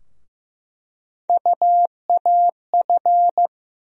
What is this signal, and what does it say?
Morse code, heard as a keyed tone, about 15 wpm, 700 Hz. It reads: UAF
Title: USA Call Sign Postfixes - 3 Letters 15wpm